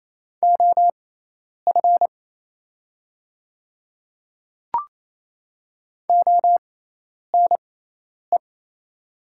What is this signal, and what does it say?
Morse code, heard as a keyed tone, about 28 wpm, spaced as for 10 wpm, 700 Hz. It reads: OF ONE
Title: 6x Character Spacing - Top 300 Words - 28wpm